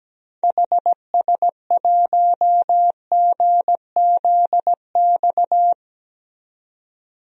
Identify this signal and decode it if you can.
Morse code, heard as a keyed tone, about 17 wpm, 700 Hz. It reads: HS1GZX
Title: DXCC-100 Call Signs - 6 Characters - 17wpm